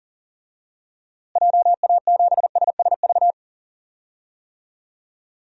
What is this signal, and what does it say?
Morse code, heard as a keyed tone, about 40 wpm, 700 Hz. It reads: JA7SSV
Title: DXCC-100 Call Signs - 6 Characters - Repeated 3x - 40wpm